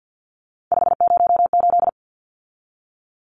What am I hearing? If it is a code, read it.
Morse code, heard as a keyed tone, about 49 wpm, 700 Hz. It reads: 508
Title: Numbers - 3-Digit 50wpm